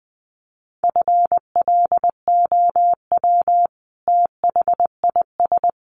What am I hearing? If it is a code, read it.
Morse code, heard as a keyed tone, about 20 wpm, 700 Hz. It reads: FLOW THIS
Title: Sentences from Top 1200 Words 20wpm